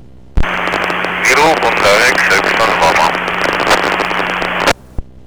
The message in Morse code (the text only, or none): none